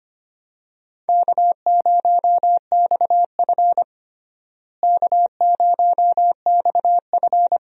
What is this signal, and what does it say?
Morse code, heard as a keyed tone, about 25 wpm, 700 Hz. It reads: K0XF K0XF
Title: USA Call Signs - 4 Characters - Repeated 2x - 25wpm